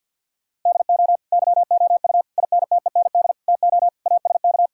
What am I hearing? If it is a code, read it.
Morse code, heard as a keyed tone, about 46 wpm, 700 Hz. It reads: DO YOU INTEND TO ASK
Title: Sentences from Top 1600 Words 50wpm